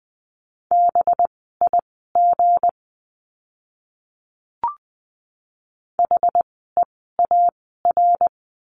Morse code, heard as a keyed tone, about 20 wpm, spaced as for 15 wpm, 700 Hz. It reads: BIG HEAR
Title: 206 Lesson 33f - Letter j - Words 20wpm